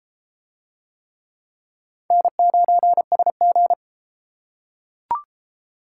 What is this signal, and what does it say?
Morse code, heard as a keyed tone, about 33 wpm, 700 Hz. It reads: N9SG